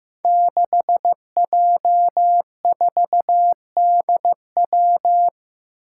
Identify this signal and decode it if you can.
Morse code, heard as a keyed tone, about 15 wpm, 700 Hz. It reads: 6J4DW
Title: Five Challenging Letter-Number Combinations 15wpm